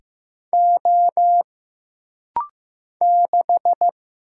Morse code, heard as a keyed tone, about 15 wpm, 700 Hz. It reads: O 6
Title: Single Letter-Number Alt - Rapid-Fire 15wpm